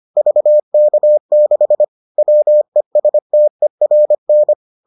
Morse code, heard as a keyed tone, about 25 wpm, 600 Hz. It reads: VK6 WESTERN